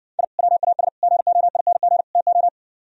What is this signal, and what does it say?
Morse code, heard as a keyed tone, about 69 wpm, 700 Hz. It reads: I WAS GOING TO